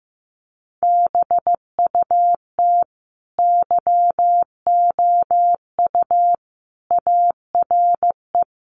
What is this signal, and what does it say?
Morse code, heard as a keyed tone, about 15 wpm, 700 Hz. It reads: BUT YOU ARE
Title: Sentences from Top 1200 Words 15wpm